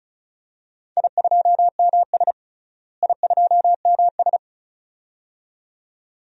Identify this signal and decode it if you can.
Morse code, heard as a keyed tone, about 35 wpm, 700 Hz. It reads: I2MS I2MS
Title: DXCC-100 Call Signs - 4 Characters - Repeated 2x - 35wpm